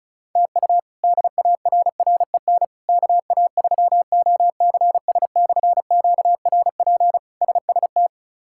Morse code, heard as a keyed tone, about 34 wpm, 700 Hz. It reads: TU DARREN KA3OCS/QRP SST
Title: K1USN SST QSO Practice - 35wpm